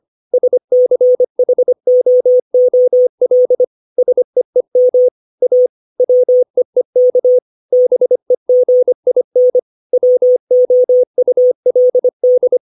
Morse code, heard as a keyed tone, about 25 wpm, 500 Hz. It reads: SCHOOL SEEM A WEEK BEGIN WOULD
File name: 632 words / 632 Words 25WPM